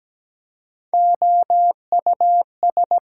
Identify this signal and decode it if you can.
Morse code, heard as a keyed tone, about 17 wpm, 700 Hz. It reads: OUS
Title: Prefixes and Suffixes 17wpm